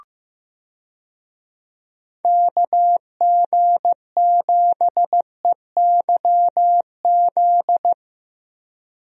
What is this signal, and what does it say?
Morse code, heard as a keyed tone, about 15 wpm, 700 Hz. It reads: KG7EYZ
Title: USA Call Signs - 6 Characters - Repeated 3x - 15wpm